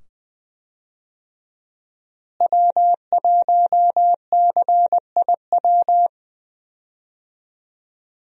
Morse code, heard as a keyed tone, about 20 wpm, 700 Hz. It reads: W1CIW
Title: USA Call Signs - 5 Characters - Repeated 2x - 20wpm